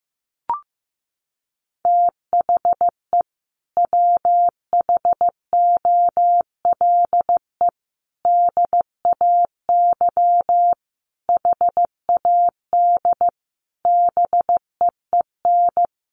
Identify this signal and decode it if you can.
Morse code, heard as a keyed tone, about 15 wpm, 700 Hz. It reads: THE WHOLE DAY HAD BEEN
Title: Sentences from Top 500 Words 15wpm